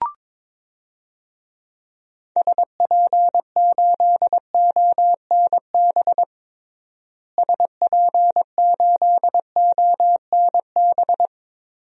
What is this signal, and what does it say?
Morse code, heard as a keyed tone, about 22 wpm, 700 Hz. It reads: SP8ONB SP8ONB
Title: DXCC-100 Call Signs - 6 Characters - Repeated 3x - 22wpm